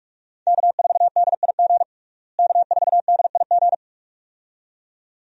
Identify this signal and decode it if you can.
Morse code, heard as a keyed tone, about 43 wpm, 700 Hz. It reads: K4DIG K4DIG
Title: USA Call Signs - 5 Characters - Repeated 2x - 45wpm